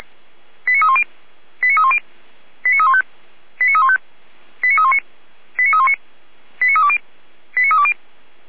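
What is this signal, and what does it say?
Five-tone selective call (ZVEI), heard as quick runs of five tones, tones of about 70 ms. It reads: 89310 89310 89315 89315 89319 89319 89320 89320
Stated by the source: ZVEI five-tone call received by SDR, from the feuerwehr-orlamuende.org server.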